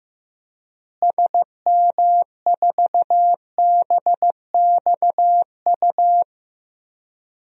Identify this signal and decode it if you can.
Morse code, heard as a keyed tone, about 15 wpm, 700 Hz. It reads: SM4BXU
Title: DXCC-100 Call Signs - 6 Characters - Repeated 3x - 15wpm